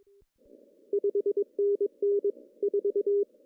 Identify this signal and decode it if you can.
Morse code, heard as a keyed tone, about 22 wpm, 400 Hz. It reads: 5NN 4
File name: CW / cqww-cw-k5tr